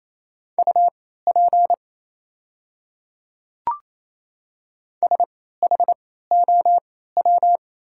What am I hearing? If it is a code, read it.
Morse code, heard as a keyed tone, about 27 wpm, spaced as for 16 wpm, 690 Hz. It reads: UP SHOW
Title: 3x Character Spacing - Top 200 Words - 28wpm